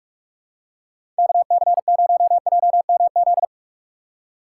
Morse code, heard as a keyed tone, about 44 wpm, 700 Hz. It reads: KC0JMZ
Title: USA Call Signs - 6 Characters - 4x Character Spacing - 45wpm